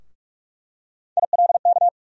Morse code, heard as a keyed tone, about 42 wpm, 700 Hz. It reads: ILK
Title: USA Call Sign Postfixes - 3 Letters 45wpm